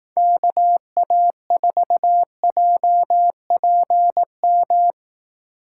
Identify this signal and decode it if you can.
Morse code, heard as a keyed tone, about 18 wpm, 700 Hz. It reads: KA4JPM
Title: Speed Racing - USA 2x3 Call Signs 15wpm